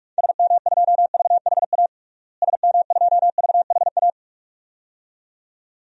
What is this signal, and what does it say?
Morse code, heard as a keyed tone, about 46 wpm, 700 Hz. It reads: SM2VHA SM2VHA
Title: DXCC-100 Call Signs - 6 Characters - Repeated 2x - 45wpm